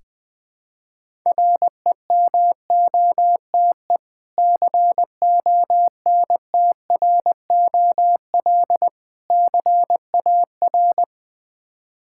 Morse code, heard as a keyed tone, about 20 wpm, 700 Hz. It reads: REMOTE CONTROL CAR
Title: Difficult Words with Hints - 20wpm